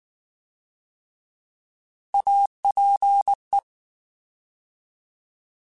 Morse code, heard as a keyed tone, about 19 wpm, 800 Hz. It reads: APE